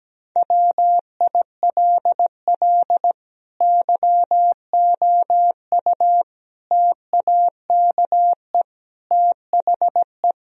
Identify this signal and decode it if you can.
Morse code, heard as a keyed tone, about 17 wpm, 700 Hz. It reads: WILL YOU TAKE THE
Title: Sentences from Top 1000 Words - Review Part 3 - 17wpm